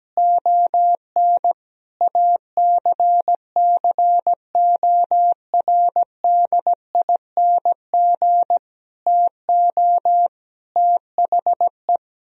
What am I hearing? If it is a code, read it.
Morse code, heard as a keyed tone, about 17 wpm, 700 Hz. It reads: ON ACCORDING TO THE
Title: Sentences from Top 1000 Words - Review Part 4 - 17wpm